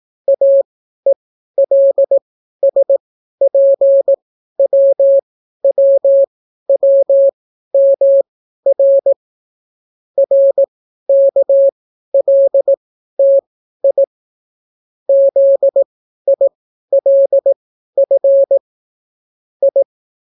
Morse code, heard as a keyed tone, about 18 wpm, spaced as for 12 wpm, 550 Hz. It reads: AELSPWWWMR RKLTI ZILF I